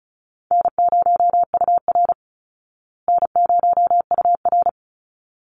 Morse code, heard as a keyed tone, about 34 wpm, 700 Hz. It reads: N0UR N0UR
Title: Call Signs Repeated 3x - 35wpm